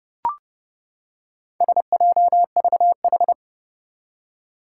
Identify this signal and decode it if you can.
Morse code, heard as a keyed tone, about 30 wpm, 700 Hz. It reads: SJVH
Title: Four Challenging Letter-Number Combinations 30wpm